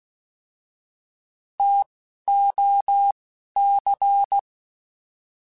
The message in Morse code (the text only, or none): TOC